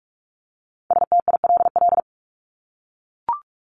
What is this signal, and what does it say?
Morse code, heard as a keyed tone, about 44 wpm, 700 Hz. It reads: STILL
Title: Top 500 Words 45wpm